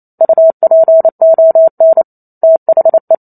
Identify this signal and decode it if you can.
Morse code, heard as a keyed tone, about 28 wpm, 650 Hz. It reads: UPON THE
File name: Chapter 2 - 28 WPM